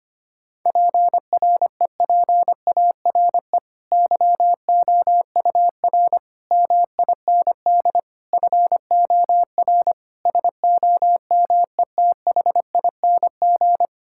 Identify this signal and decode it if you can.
Morse code, heard as a keyed tone, about 25 wpm, 700 Hz. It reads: PREPARE YOUR MIND FOR SOMETHING
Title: Sentences from Top 1000 Words - Review Part 6 - 25wpm